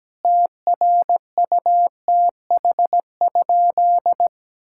Morse code, heard as a keyed tone, about 17 wpm, 700 Hz. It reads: TRUTH?
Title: Sentences from Top 1600 Words 17wpm